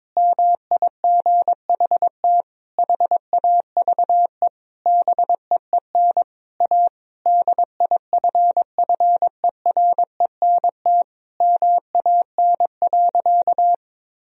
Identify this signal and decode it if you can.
Morse code, heard as a keyed tone, about 22 wpm, 700 Hz. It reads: MIGHT HAVE BEEN A DIFFERENT MAN.
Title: Sentences from Top 1000 Words - Review Part 3 - 22wpm